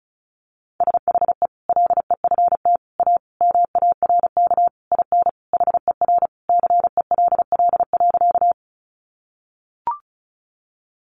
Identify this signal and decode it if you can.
Morse code, heard as a keyed tone, about 34 wpm, 700 Hz. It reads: SHE LEFT A MARK IN HER CELL.
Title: Sentences from Top 1100 Words 35wpm